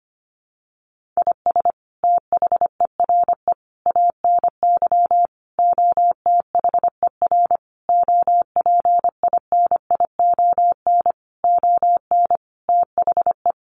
Morse code, heard as a keyed tone, about 25 wpm, 700 Hz. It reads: IS THERE ANY OTHER OPINION ON THE
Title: Sentences from Top 1200 Words 25wpm